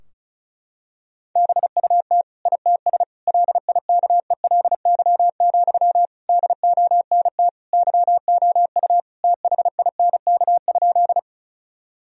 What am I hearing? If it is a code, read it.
Morse code, heard as a keyed tone, about 34 wpm, 700 Hz. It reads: BUT ITS LIKELY, DONT YOU THINK?
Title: Sentences from Top 1000 Words - Review Part 4 - 35wpm